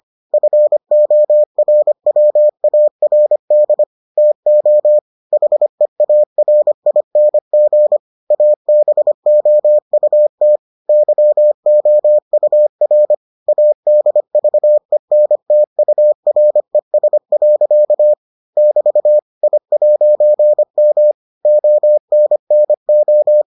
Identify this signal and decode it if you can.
Morse code, heard as a keyed tone, about 25 wpm, 600 Hz. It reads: FORWARD TO HEARING ABOUT YOUR ADVENTURES. <BT> I'M ONNO